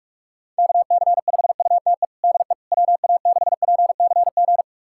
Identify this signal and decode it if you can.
Morse code, heard as a keyed tone, about 41 wpm, 690 Hz. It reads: KC5UTE DE WA6PCG
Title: USA Callsign DE Callsign 45wpm